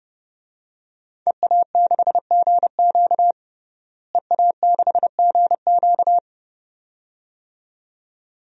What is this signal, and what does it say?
Morse code, heard as a keyed tone, about 30 wpm, 700 Hz. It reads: EA6GQ EA6GQ
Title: DXCC-100 Call Signs - 5 Characters - Repeated 2x - 30wpm